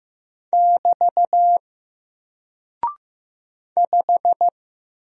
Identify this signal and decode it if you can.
Morse code, heard as a keyed tone, about 15 wpm, 700 Hz. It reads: <BT> 5
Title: Single Character (Letter:Number:Punctuation:Prosign) Alt 15wpm